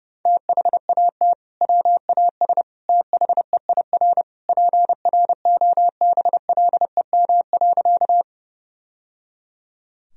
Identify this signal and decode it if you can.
Morse code, heard as a keyed tone, about 30 wpm, 700 Hz. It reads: THAT WAS THEIR PROBLEM.
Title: Sentences from Top 1000 Words - Review Part 2 - 30wpm